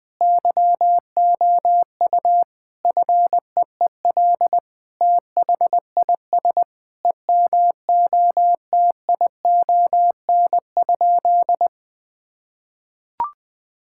Morse code, heard as a keyed tone, about 20 wpm, 700 Hz. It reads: YOU FEEL THIS EMOTION?
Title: Sentences from Top 1600 Words 20wpm